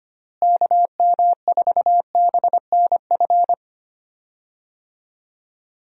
Morse code, heard as a keyed tone, about 25 wpm, 700 Hz. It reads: KM4BNF
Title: USA Call Signs - 6 Characters - Repeated 3x - 25wpm